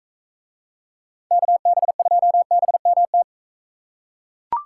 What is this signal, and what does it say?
Morse code, heard as a keyed tone, about 42 wpm, 700 Hz. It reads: KB2BMT